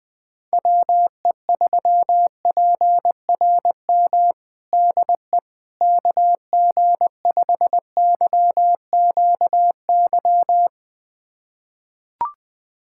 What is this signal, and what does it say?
Morse code, heard as a keyed tone, about 20 wpm, 700 Hz. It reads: WE3PRM DE KG5YQY
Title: USA Callsign DE Callsign - 6 Characters - 20wpm